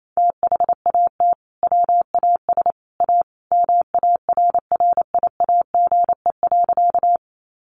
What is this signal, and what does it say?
Morse code, heard as a keyed tone, about 28 wpm, 700 Hz. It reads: THAT WAS A MARRIAGE.
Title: Sentences from Top 800 Words 28wpm